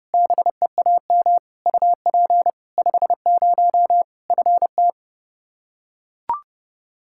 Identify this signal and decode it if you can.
Morse code, heard as a keyed tone, about 30 wpm, 700 Hz. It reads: BEAM UP 50 FT
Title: Practice QSO - Exchange 2 - Info - 30wpm